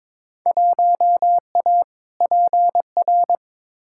Morse code, heard as a keyed tone, about 22 wpm, 700 Hz. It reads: 1A PR
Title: Field Day Exchanges Encore - 22wpm